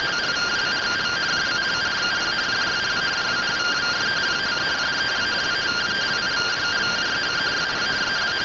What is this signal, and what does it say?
Two-tone teleprinter signal, about 170 Hz shift.
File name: findings / MR AXIAL 33,99mm 1.gen-19 rtty rx